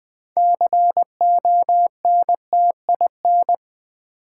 Morse code, heard as a keyed tone, about 20 wpm, 700 Hz. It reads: CONTIN
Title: Build A Word - 3 Letters At A Time - 9-Letter Words - 20wpm